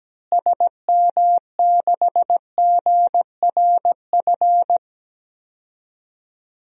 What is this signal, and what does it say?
Morse code, heard as a keyed tone, about 17 wpm, 700 Hz. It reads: SM6GRF
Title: DXCC-100 Call Signs - 6 Characters - Repeated 2x - 17wpm